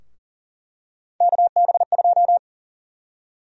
Morse code, heard as a keyed tone, about 40 wpm, 700 Hz. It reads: KB2
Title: USA Call Sign Prefixes - Repeated 2x - 40wpm